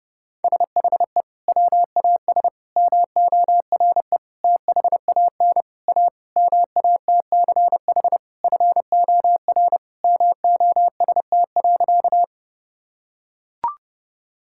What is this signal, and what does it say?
Morse code, heard as a keyed tone, about 30 wpm, 700 Hz. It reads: SHE WAS MORE THAN A MATCH FOR MOST.
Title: Sentences from Top 1600 Words 30wpm